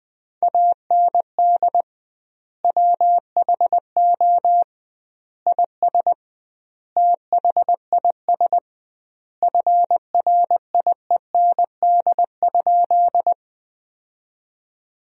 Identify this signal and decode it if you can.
Morse code, heard as a keyed tone, about 20 wpm, 700 Hz. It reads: AND WHO IS THIS FRIEND?
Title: 2x Word-Spacing - Sentences from Top 300 Words - 20wpm